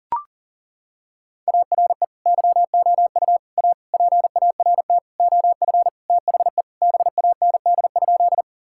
Morse code, heard as a keyed tone, about 37 wpm, 700 Hz. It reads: ARE YOU A PART OF THE BAND?
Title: Sentences from Top 1600 Words 40wpm